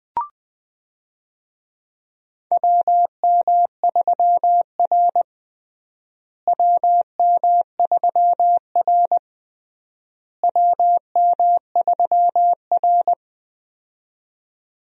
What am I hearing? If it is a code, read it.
Morse code, heard as a keyed tone, about 20 wpm, 700 Hz. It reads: WM3R WM3R WM3R
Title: USA Call Signs - 4 Characters - Repeated 3x - 20wpm